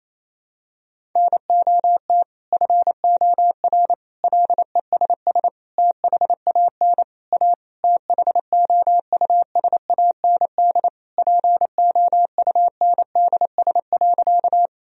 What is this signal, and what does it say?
Morse code, heard as a keyed tone, about 28 wpm, 700 Hz. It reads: NOT FOR LESS THAN A THOUSAND POUNDS.
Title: Sentences from Top 1000 Words - Review Part 3 - 28wpm